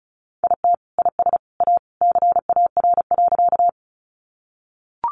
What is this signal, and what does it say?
Morse code, heard as a keyed tone, about 34 wpm, 700 Hz. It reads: IT IS A CAR.
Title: Easy Phrases 35wpm